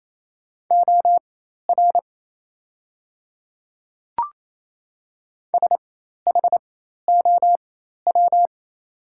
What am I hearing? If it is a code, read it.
Morse code, heard as a keyed tone, about 27 wpm, spaced as for 13 wpm, 700 Hz. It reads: OR SHOW
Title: 4x Character Spacing - Top 300 Words - 28wpm